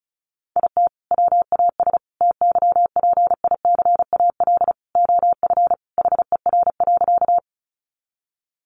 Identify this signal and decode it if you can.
Morse code, heard as a keyed tone, about 34 wpm, 700 Hz. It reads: IT WAS TYPICAL OF HER.
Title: Sentences from Top 1100 Words 35wpm